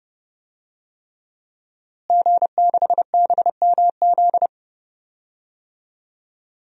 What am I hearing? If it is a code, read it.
Morse code, heard as a keyed tone, about 31 wpm, 700 Hz. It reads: G6BMZ
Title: DXCC-100 Call Signs - 5 Characters - Repeated 3x - 30wpm